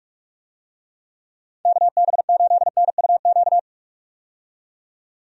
Morse code, heard as a keyed tone, about 46 wpm, 700 Hz. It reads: KB8NUQ
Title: USA Call Signs - 6 Characters - 45wpm